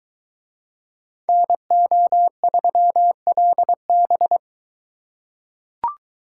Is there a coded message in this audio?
Yes: Morse code from 1.3 to 4.4 s, heard as a keyed tone, about 21 wpm, 700 Hz. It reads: NO3LB